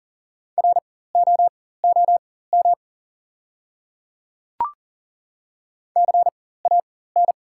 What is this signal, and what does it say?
Morse code, heard as a keyed tone, about 40 wpm, spaced as for 19 wpm, 690 Hz. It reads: ROOM CAN
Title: 4x Character Spacing - Top 300 Words - 40wpm